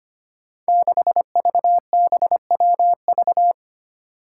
Morse code, heard as a keyed tone, about 25 wpm, 700 Hz. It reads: 6VBWV
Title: Five Challenging Letter-Number Combinations 25wpm